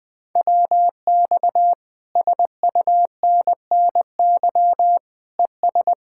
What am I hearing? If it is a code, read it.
Morse code, heard as a keyed tone, about 20 wpm, 700 Hz. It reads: WX SUNNY ES
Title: Speed-Racing Common QSO Phrases - 17wpm